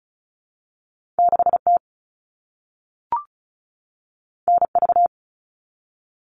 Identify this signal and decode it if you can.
Morse code, heard as a keyed tone, about 36 wpm, 700 Hz. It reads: 6T NV
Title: All Cut Numbers - 2-Digit 35wpm